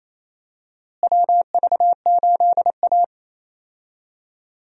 Morse code, heard as a keyed tone, about 28 wpm, 700 Hz. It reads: WV8A
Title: USA Call Signs - 4 Characters - Repeated 3x - 28wpm